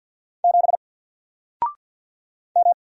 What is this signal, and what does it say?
Morse code, heard as a keyed tone, about 50 wpm, 700 Hz. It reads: 7 M
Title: Single Character (Letter:Number:Punctuation:Prosign) 50wpm